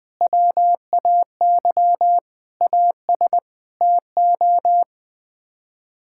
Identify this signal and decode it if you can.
Morse code, heard as a keyed tone, about 21 wpm, 700 Hz. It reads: WAY AS TO
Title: Sets of 6 Words - 20wpm